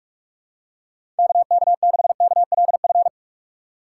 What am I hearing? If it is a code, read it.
Morse code, heard as a keyed tone, about 46 wpm, 700 Hz. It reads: KK6KLF
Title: USA Call Signs - 6 Characters - 4x Character Spacing - 45wpm